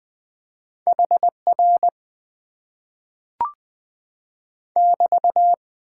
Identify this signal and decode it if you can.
Morse code, heard as a keyed tone, about 20 wpm, 700 Hz. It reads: HR <BT>
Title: Top 90 QSO Elements - 8x Character Spacing - 20wpm